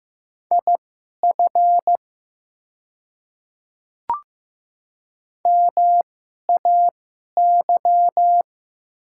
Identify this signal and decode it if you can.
Morse code, heard as a keyed tone, about 15 wpm, spaced as for 11 wpm, 700 Hz. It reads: IF MAY
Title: 2x Character Spacing - Top 100 Words - 15wpm